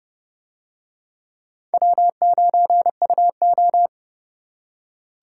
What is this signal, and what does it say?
Morse code, heard as a keyed tone, about 28 wpm, 700 Hz. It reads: W9UO